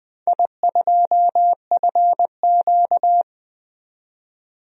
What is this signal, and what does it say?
Morse code, heard as a keyed tone, about 20 wpm, 700 Hz. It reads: I2FQ
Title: DXCC-100 Call Signs - 4 Characters - 20wpm